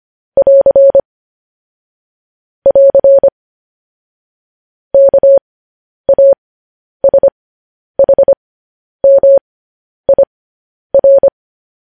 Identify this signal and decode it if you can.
Morse code, heard as a keyed tone, about 25 wpm, spaced as for 10 wpm, 550 Hz. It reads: <AR> <AR> KASHMIR
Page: news-headlines-in-morse-code-at-10-wpm